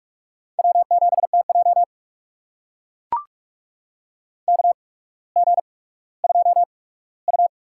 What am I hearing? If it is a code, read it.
Morse code, heard as a keyed tone, about 43 wpm, spaced as for 19 wpm, 700 Hz. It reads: W7TJ KG2U
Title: USA Call Signs - 4 Characters - 8x Character Spacing - 45wpm